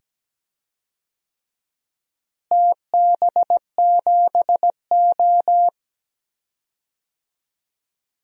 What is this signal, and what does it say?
Morse code, heard as a keyed tone, about 17 wpm, 700 Hz. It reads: TB7O
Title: DXCC-100 Call Signs - 4 Characters - Repeated 2x - 17wpm